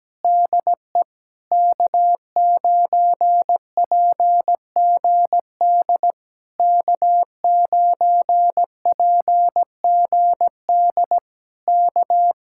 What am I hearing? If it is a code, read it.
Morse code, heard as a keyed tone, about 17 wpm, 700 Hz. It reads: DE K9PGD K9PGD K
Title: Speed-Racing Common QSO Phrases Encore - 17wpm